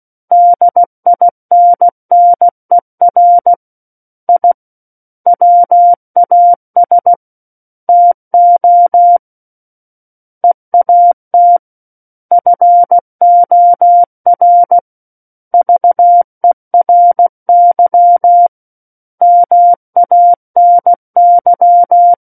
Morse code, heard as a keyed tone, about 16 wpm, 700 Hz. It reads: DINNER I WAS TO EAT FOR VERY MANY
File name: Book One / 07-Chapter 7 - 16 WPM